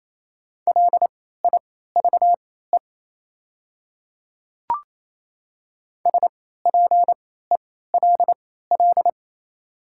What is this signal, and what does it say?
Morse code, heard as a keyed tone, about 27 wpm, spaced as for 16 wpm, 690 Hz. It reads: LIVE SPELL
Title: 3x Character Spacing - Top 200 Words - 28wpm